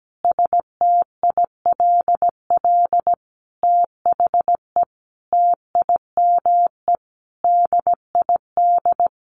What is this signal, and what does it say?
Morse code, heard as a keyed tone, about 17 wpm, 700 Hz. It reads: STILL THE TIME DID
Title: Sentences from Top 1500 Words 17wpm